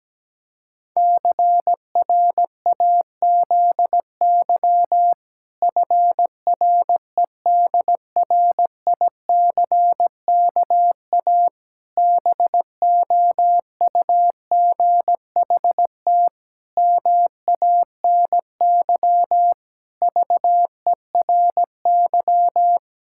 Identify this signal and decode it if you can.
Morse code, heard as a keyed tone, about 17 wpm, 700 Hz. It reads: CRAZY FREDRICKA BOUGHT MANY VERY
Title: Pangram Challenge v2 - 17wpm